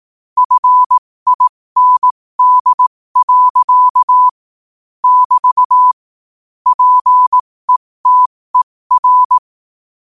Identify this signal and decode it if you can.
Morse code, heard as a keyed tone, about 18 wpm, spaced as for 15 wpm, 1000 Hz. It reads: FIND. <BT> PETER
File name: qotdfiles / 15wpm